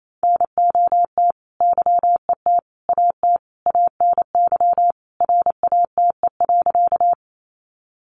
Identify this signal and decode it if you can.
Morse code, heard as a keyed tone, about 28 wpm, 700 Hz. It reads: NOT YET AT ANY RATE.
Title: Sentences from Top 700 Words 28wpm